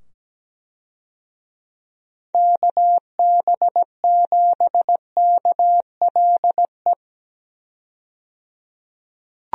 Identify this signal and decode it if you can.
Morse code, heard as a keyed tone, about 17 wpm, 700 Hz. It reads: KB7KLE
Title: USA Call Signs - 6 Characters - 17wpm